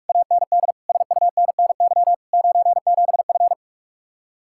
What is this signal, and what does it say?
Morse code, heard as a keyed tone, about 56 wpm, 700 Hz. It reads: AND SUNNY 97F